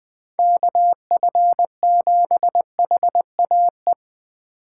Morse code, heard as a keyed tone, about 20 wpm, 700 Hz. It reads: KF7HAE